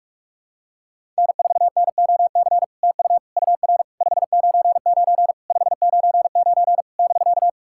Morse code, heard as a keyed tone, about 44 wpm, 690 Hz. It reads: N4NOC TU UR 599 599 <BK>
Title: SOTA QSO Practice 45wpm